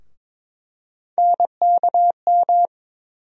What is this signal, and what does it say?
Morse code, heard as a keyed tone, about 22 wpm, 700 Hz. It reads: NKM
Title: USA Call Sign Postfixes - 3 Letters 22wpm